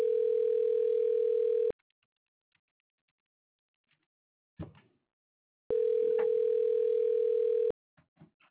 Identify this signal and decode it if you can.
Ringback tone (North America), 440 + 480 Hz.